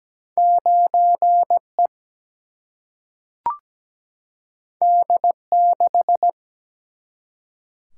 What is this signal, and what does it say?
Morse code, heard as a keyed tone, about 17 wpm, 700 Hz. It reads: 9 E D 6
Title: Single-Letter Single-Number 17wpm